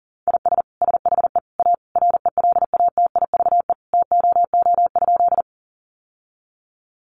Morse code, heard as a keyed tone, about 40 wpm, 700 Hz. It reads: IS SHE A RELATIVE TOO?
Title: Sentences from Top 1400 Words 40wpm